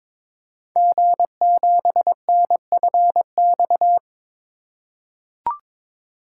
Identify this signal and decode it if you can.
Morse code, heard as a keyed tone, about 22 wpm, 690 Hz. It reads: G7NFX
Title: Code Groups - 5 Letter:Numbers - 22wpm